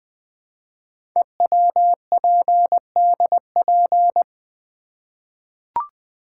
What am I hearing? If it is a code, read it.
Morse code, heard as a keyed tone, about 20 wpm, 700 Hz. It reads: EWPDP
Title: Code Groups - 5 Letters - 20wpm